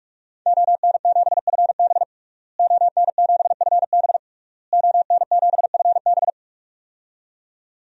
Morse code, heard as a keyed tone, about 42 wpm, 700 Hz. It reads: ON7FB ON7FB ON7FB
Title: DXCC-100 Call Signs - 5 Characters - Repeated 3x - 45wpm